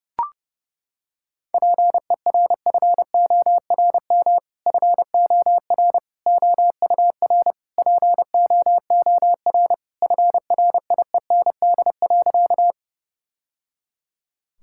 Morse code, heard as a keyed tone, about 30 wpm, 700 Hz. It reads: PERFORM FOR OUR POOR FRIEND.
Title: Sentences from Top 1000 Words - Review Part 5 - 30wpm